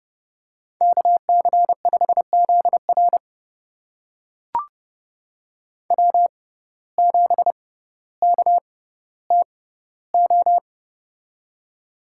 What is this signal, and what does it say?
Morse code, heard as a keyed tone, about 30 wpm, spaced as for 15 wpm, 700 Hz. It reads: KC5ZR W7KTO
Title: USA Call Signs - 6x Character Spacing - 30wpm